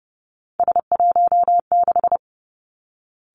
Morse code, heard as a keyed tone, about 30 wpm, 700 Hz. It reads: S16
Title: Three Challenging Letter-Number Combinations 30wpm